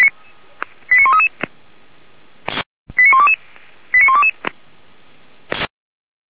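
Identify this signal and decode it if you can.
Five-tone selective call (ZVEI), heard as quick runs of five tones, tones of about 80 ms. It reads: 89133 89122 89122
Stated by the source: ZVEI five-tone call received by SDR.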